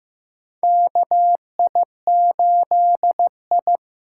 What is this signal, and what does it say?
Morse code, heard as a keyed tone, about 15 wpm, 700 Hz. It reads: KI8I
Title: K1USN SST QSO Practice - 15wpm